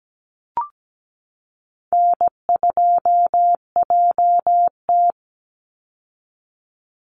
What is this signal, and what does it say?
Morse code, heard as a keyed tone, about 17 wpm, 700 Hz. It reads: N2JT
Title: Easy Call Signs Repeated 3x - 17wpm